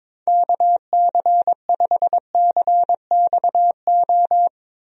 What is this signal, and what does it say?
Morse code, heard as a keyed tone, about 21 wpm, 700 Hz. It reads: KC5CXO
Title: USA Call Signs - 6 Characters - Repeated 2x - 22wpm